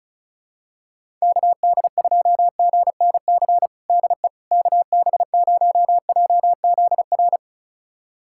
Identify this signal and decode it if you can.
Morse code, heard as a keyed tone, about 34 wpm, 700 Hz. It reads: KD2GNC DE KB0JZR
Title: USA Callsign DE Callsign - 6 Characters - 35wpm